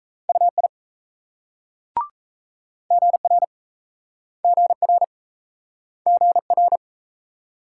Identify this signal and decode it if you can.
Morse code, heard as a keyed tone, about 38 wpm, 700 Hz. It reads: UI GR GR GR